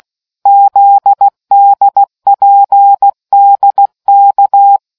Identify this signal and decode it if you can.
Morse code, heard as a keyed tone, about 16 wpm, 800 Hz. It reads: ZDPDK